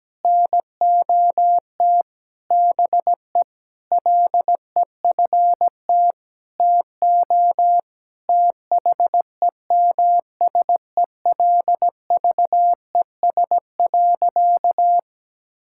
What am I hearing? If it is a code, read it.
Morse code, heard as a keyed tone, about 17 wpm, 700 Hz. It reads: NOT BE LEFT TO THEMSELVES.
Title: Sentences from Top 1000 Words - Review Part 5 - 17wpm